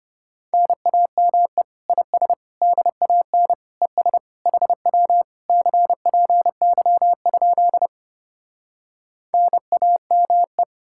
Morse code, heard as a keyed tone, about 33 wpm, 700 Hz. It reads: NAME IS DAN ES HW CPY? NAME